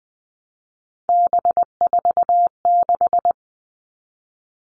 Morse code, heard as a keyed tone, about 20 wpm, 700 Hz. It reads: B46
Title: All Cut Numbers - 3-Digit 20wpm